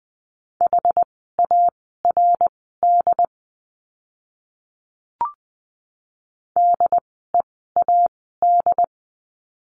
Morse code, heard as a keyed tone, about 20 wpm, spaced as for 14 wpm, 700 Hz. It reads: HARD DEAD
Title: 075 Lesson 13f - Letter L - Words 20wpm